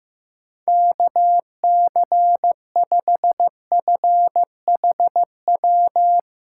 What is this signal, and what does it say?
Morse code, heard as a keyed tone, about 15 wpm, 700 Hz. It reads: KC5FHW
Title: USA Callsign DE Callsign 15wpm